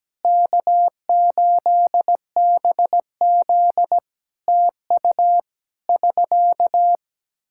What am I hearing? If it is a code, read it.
Morse code, heard as a keyed tone, about 17 wpm, 700 Hz. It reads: K8BZ TU <SK>
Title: K1USN SST QSO Practice 17wpm